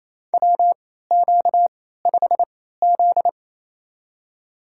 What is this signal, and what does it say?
Morse code, heard as a keyed tone, about 28 wpm, spaced as for 16 wpm, 700 Hz. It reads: WQ5Z
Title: USA Call Signs - 3x Character Spacing - 28wpm